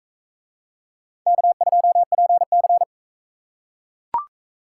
Morse code, heard as a keyed tone, about 38 wpm, 700 Hz. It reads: K2PC